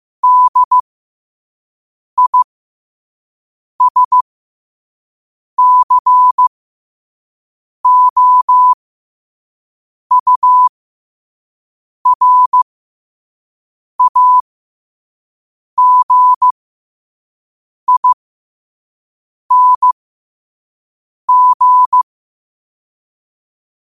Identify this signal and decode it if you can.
Morse code, heard as a keyed tone, about 15 wpm, spaced as for 5 wpm, 1000 Hz. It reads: DISCOURAGING
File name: qotdfiles / 5wpm